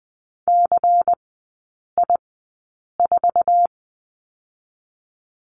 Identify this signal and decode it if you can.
Morse code, heard as a keyed tone, about 20 wpm, spaced as for 8 wpm, 700 Hz. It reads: CI4
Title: 093 Lesson 16e - Letter c - Three Characters 20wpm